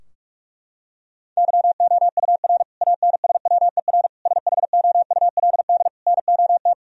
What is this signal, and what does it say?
Morse code, heard as a keyed tone, about 46 wpm, 700 Hz. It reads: YOUR ANSWER SHOULD NOT
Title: Sentences from Top 1000 Words - Review Part 3 - 45wpm